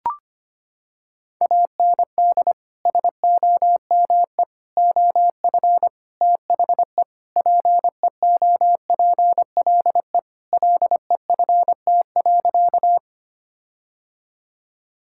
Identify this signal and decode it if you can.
Morse code, heard as a keyed tone, about 25 wpm, 700 Hz. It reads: AND SOME OF THE PEOPLE LEFT.
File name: practice / 25wpm / Sentences with 4-Word Phrases - 25wpm